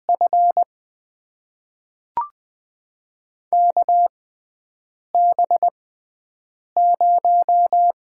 Morse code, heard as a keyed tone, about 20 wpm, spaced as for 7 wpm, 700 Hz. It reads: F KB0
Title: USA Call Signs - 6 Characters - 6x Character Spacing - 20wpm